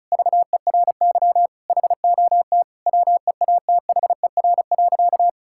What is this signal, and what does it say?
Morse code, heard as a keyed tone, about 34 wpm, 700 Hz. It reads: VERY HOT WEATHER.
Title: Sentences from Top 1000 Words - Review Part 5 - 35wpm